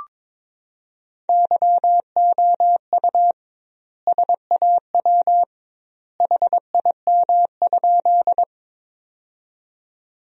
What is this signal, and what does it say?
Morse code, heard as a keyed tone, about 22 wpm, 700 Hz. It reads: YOU SAW HIM?
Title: 2x Word-Spacing - Sentences from Top 300 Words - 22wpm